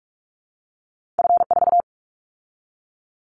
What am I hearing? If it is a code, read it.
Morse code, heard as a keyed tone, about 44 wpm, 700 Hz. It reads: F4
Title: Callsign Prefixes for Top 50 DXCC - 45wpm